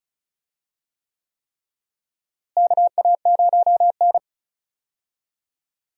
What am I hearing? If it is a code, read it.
Morse code, heard as a keyed tone, about 36 wpm, 700 Hz. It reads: KA0N
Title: USA Call Signs - 4 Characters - Repeated 3x - 35wpm